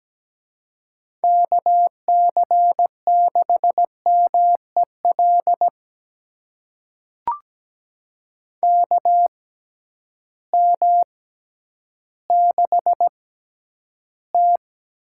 Morse code, heard as a keyed tone, about 17 wpm, spaced as for 10 wpm, 700 Hz. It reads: KC6MEL KM6T
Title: USA Call Signs - 6 Characters - 6x Character Spacing - 17wpm